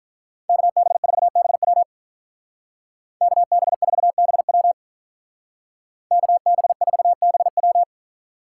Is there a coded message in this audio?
Yes: Morse code from 0.5 to 7.8 s, heard as a keyed tone, about 47 wpm, 700 Hz. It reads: KB4BW KB4BW KB4BW